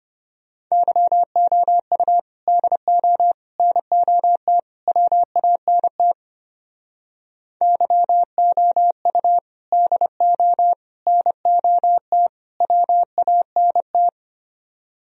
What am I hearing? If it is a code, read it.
Morse code, heard as a keyed tone, about 31 wpm, 700 Hz. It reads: YOU DO NOT WANT YOU DO NOT WANT
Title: Speed-Racing - Sets of 4 Words 25wpm